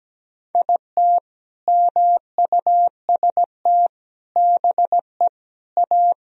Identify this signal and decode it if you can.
Morse code, heard as a keyed tone, about 17 wpm, 700 Hz. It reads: IT MUST BE A
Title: Sentences from Top 500 Words and Abbreviations 17wpm